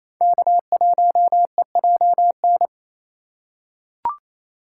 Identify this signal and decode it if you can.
Morse code, heard as a keyed tone, about 28 wpm, 700 Hz. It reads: K1EJN
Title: USA Call Signs - 5 Characters - 6x Character Spacing - 28wpm